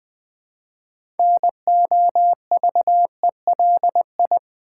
Morse code, heard as a keyed tone, about 20 wpm, 700 Hz. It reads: NOVELI
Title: Build A Word - 2 Letters At A Time - 8-Letter Words - 20wpm